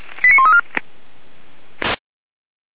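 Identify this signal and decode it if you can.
Five-tone selective call (ZVEI), heard as quick runs of five tones, tones of about 80 ms. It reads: 89134
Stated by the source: ZVEI five-tone call 89132 received with an SDR.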